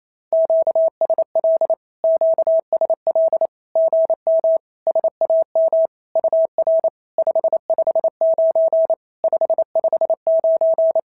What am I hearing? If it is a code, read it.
Morse code, heard as a keyed tone, about 28 wpm, 650 Hz. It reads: QSL QSL GM SAM UR 559 559
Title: SOTA QSO Practice 28wpm